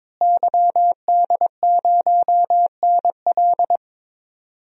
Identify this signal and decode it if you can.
Morse code, heard as a keyed tone, about 22 wpm, 700 Hz. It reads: YD0NL
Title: DXCC-100 Call Signs - 5 Characters - Repeated 2x - 22wpm